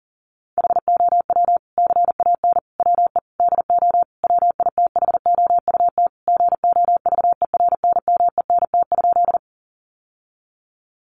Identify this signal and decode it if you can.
Morse code, heard as a keyed tone, about 40 wpm, 700 Hz. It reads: HOW CAN WE DO WITHOUT GOVERNMENT?
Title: Sentences from Top 600 Words 40wpm